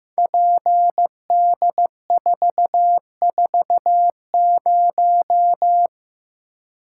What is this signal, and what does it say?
Morse code, heard as a keyed tone, about 15 wpm, 700 Hz. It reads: PD440
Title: Code Groups - 5 Letter:Numbers - 15wpm